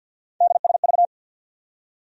Morse code, heard as a keyed tone, about 53 wpm, 700 Hz. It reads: DSV
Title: USA Call Sign Postfixes - 3 Letters 50wpm